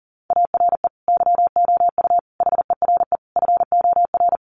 Morse code, heard as a keyed tone, about 40 wpm, 700 Hz. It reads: ARE YOU HERE FOR
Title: Sentences from Top 200 Words 40wpm